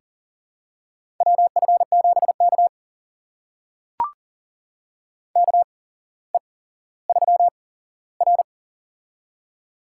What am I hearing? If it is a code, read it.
Morse code, heard as a keyed tone, about 41 wpm, spaced as for 17 wpm, 700 Hz. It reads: WF7K KE3R
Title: USA Call Signs - 4 Characters - 8x Character Spacing - 40wpm